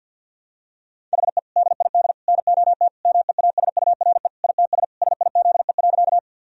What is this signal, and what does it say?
Morse code, heard as a keyed tone, about 50 wpm, 700 Hz. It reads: HE DID NOT MEASURE ITS SIZE.
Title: Sentences from Top 1000 Words - Review Part 2 - 50wpm